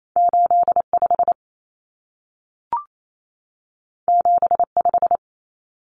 Morse code, heard as a keyed tone, about 28 wpm, 700 Hz. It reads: 85 75
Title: Common Cut Numbers - 2-Digit 28wpm